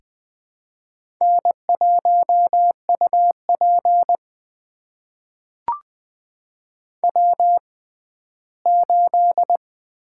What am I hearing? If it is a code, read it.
Morse code, heard as a keyed tone, about 20 wpm, spaced as for 14 wpm, 700 Hz. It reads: N1UP W8
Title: USA Call Signs - 4 Characters - 6x Character Spacing - 20wpm